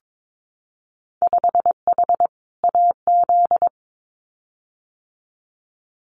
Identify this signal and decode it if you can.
Morse code, heard as a keyed tone, about 22 wpm, 700 Hz. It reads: 5H AZ
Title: Winter Field Day Exchanges Encore 22wpm